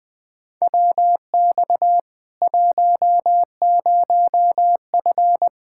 Morse code, heard as a keyed tone, about 20 wpm, 700 Hz. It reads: WX 10F
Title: Practice QSO - Exchange 2 - Info - 20wpm